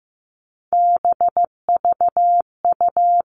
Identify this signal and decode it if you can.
Morse code, heard as a keyed tone, about 15 wpm, 700 Hz. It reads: BVU
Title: All Cut Numbers - 3-Digit 15wpm